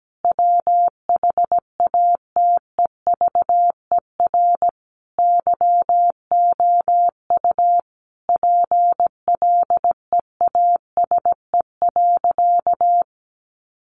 Morse code, heard as a keyed tone, about 17 wpm, 700 Hz. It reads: WHATEVER YOU PLEASE.
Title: Sentences from Top 1500 Words 17wpm